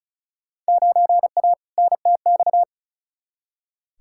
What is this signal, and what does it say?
Morse code, heard as a keyed tone, about 36 wpm, 690 Hz. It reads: 9A NTX
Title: Field Day Exchanges Encore - 35wpm